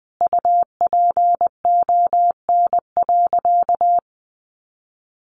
Morse code, heard as a keyed tone, about 20 wpm, 700 Hz. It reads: UPON.
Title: Sentences from Top 1100 Words 20wpm